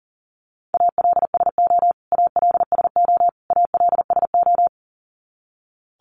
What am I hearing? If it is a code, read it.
Morse code, heard as a keyed tone, about 40 wpm, 700 Hz. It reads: ALSO ALSO ALSO
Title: Top 500 Words Repeated 40wpm